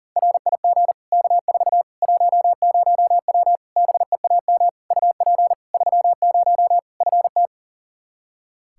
Morse code, heard as a keyed tone, about 40 wpm, 700 Hz. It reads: RIG K4 10W BEAM UP 30 FT
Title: K1USN SST QSO Practice 40wpm